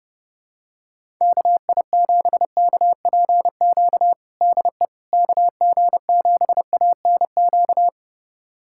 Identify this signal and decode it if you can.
Morse code, heard as a keyed tone, about 31 wpm, 700 Hz. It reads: KI7KPQ DE KG7ANQ
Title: USA Callsign DE Callsign - 6 Characters - 30wpm